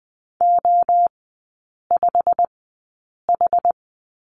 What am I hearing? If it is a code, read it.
Morse code, heard as a keyed tone, about 19 wpm, spaced as for 8 wpm, 700 Hz. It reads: O5H
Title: 121 Lesson 20e - Number 3 - Three Characters 20wpm